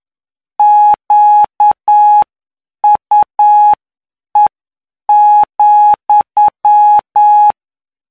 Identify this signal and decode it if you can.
Morse code, heard as a keyed tone, about 9 wpm, spaced as for 7 wpm, 840 Hz. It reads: QUE,
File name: cw / py2lcd / 19-PY2LCD-Li��o 28